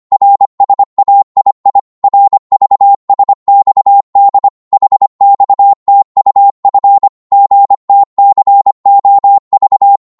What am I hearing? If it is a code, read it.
Morse code, heard as a keyed tone, about 25 wpm, 800 Hz. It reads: RSAII RVSXD HXTUF GTCOV
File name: Morse / ltr-25wpm